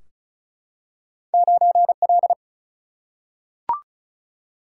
Morse code, heard as a keyed tone, about 35 wpm, 700 Hz. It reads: 9L
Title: Single-Letter Single-Number 35wpm